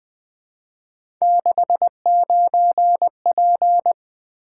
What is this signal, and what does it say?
Morse code, heard as a keyed tone, about 20 wpm, 700 Hz. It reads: 69P
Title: Three Letter-Number Combinations - 20wpm